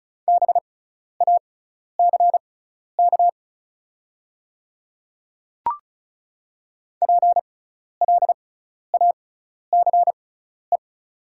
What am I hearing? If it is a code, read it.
Morse code, heard as a keyed tone, about 36 wpm, spaced as for 12 wpm, 690 Hz. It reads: BACK PLACE
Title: 6x Character Spacing - Top 200 Words - 35wpm